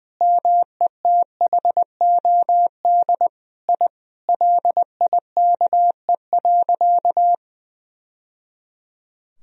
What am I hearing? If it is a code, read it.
Morse code, heard as a keyed tone, about 20 wpm, 700 Hz. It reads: METHOD I LIKE.
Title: Sentences from Top 1000 Words - Review Part 3 - 20wpm